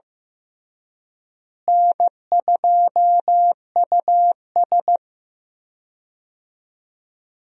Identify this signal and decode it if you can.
Morse code, heard as a keyed tone, about 15 wpm, 700 Hz. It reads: N2US
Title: USA Call Signs - 4 Characters - Repeated 3x - 15wpm